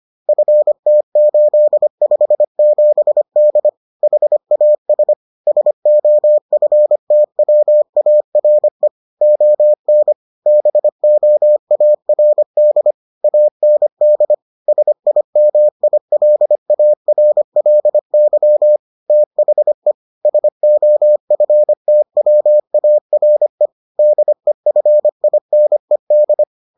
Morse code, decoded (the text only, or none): FT857D HAS SOFTWARE ON BOARD AND SIMILARLY THE SOFTWARE DEFINED